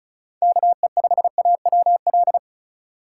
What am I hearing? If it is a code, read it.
Morse code, heard as a keyed tone, about 34 wpm, 700 Hz. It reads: KE5AWL
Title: USA Call Signs - 6 Characters - 4x Character Spacing - 35wpm